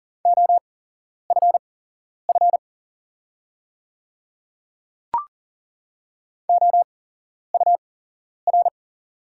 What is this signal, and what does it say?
Morse code, heard as a keyed tone, about 42 wpm, spaced as for 11 wpm, 690 Hz. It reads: OFF OUR
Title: 8x Character Spacing - Top 200 Words - 40wpm